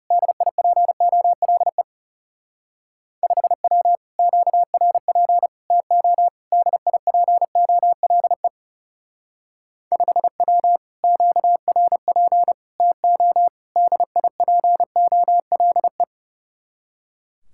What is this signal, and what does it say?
Morse code, heard as a keyed tone, about 38 wpm, 700 Hz. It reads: DIPOLE 5W QRP TO DIPOLE 5W QRP TO DIPOLE